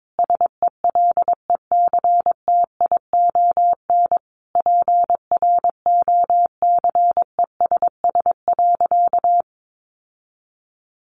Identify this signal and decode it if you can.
Morse code, heard as a keyed tone, about 22 wpm, 700 Hz. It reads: SELECTION PROCESS.
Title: Sentences from Top 1400 Words 22wpm